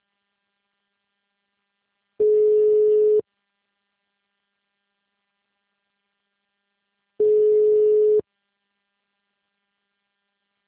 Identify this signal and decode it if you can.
Ringback tone (Europe, 425 Hz).